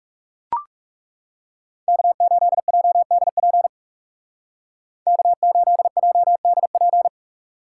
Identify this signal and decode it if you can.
Morse code, heard as a keyed tone, about 35 wpm, 700 Hz. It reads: K8JDP K8JDP
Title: Speed Racing - USA Call Signs 30wpm